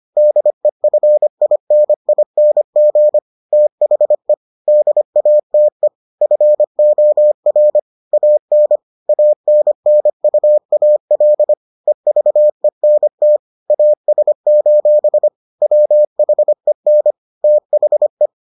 Morse code, heard as a keyed tone, about 25 wpm, 600 Hz. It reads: DEFINING THE DATE FOR AN ANNUAL EVENT AS: WHEN THE